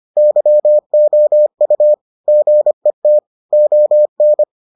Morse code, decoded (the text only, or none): YOU GET ON